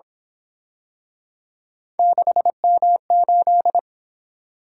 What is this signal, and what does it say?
Morse code, heard as a keyed tone, about 26 wpm, 700 Hz. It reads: 6M8
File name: Speed-Racing - Three Letter-Number Combinations - 22wpm